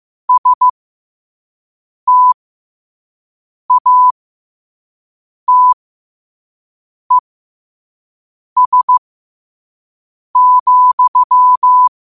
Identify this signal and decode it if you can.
Morse code, heard as a keyed tone, about 15 wpm, spaced as for 5 wpm, 1000 Hz. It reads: STATES,
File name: 2026-02-15-5wpm